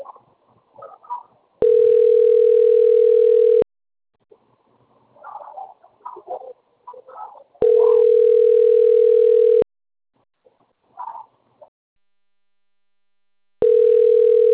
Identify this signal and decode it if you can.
Ringback tone (North America), 440 + 480 Hz.